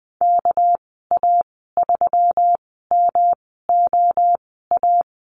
Morse code, heard as a keyed tone, about 20 wpm, spaced as for 14 wpm, 700 Hz. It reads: KA3MOA
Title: 200 Lesson 32g - Letter k - Call Signs 20wpm